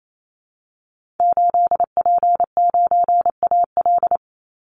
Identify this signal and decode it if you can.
Morse code, heard as a keyed tone, about 28 wpm, 700 Hz. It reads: 8P9AL
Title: Call Signs Repeated 2x - 28wpm